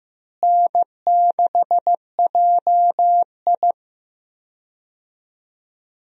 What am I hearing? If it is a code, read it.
Morse code, heard as a keyed tone, about 15 wpm, 700 Hz. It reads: N6JI
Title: USA Call Signs - 4 Characters - Repeated 3x - 15wpm